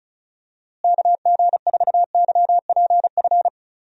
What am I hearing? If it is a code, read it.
Morse code, heard as a keyed tone, about 36 wpm, 700 Hz. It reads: KG4YPF
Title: USA Call Signs - 6 Characters - Repeated 3x - 35wpm